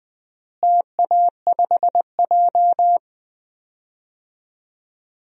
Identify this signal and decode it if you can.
Morse code, heard as a keyed tone, about 20 wpm, 700 Hz. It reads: TA5J
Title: DXCC-100 Call Signs - 4 Characters - Repeated 3x - 20wpm